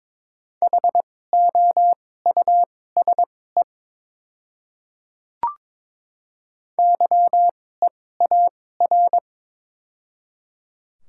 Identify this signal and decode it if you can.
Morse code, heard as a keyed tone, about 22 wpm, spaced as for 16 wpm, 700 Hz. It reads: HOUSE YEAR
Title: 2x Character Spacing - Top 500 Words - 22wpm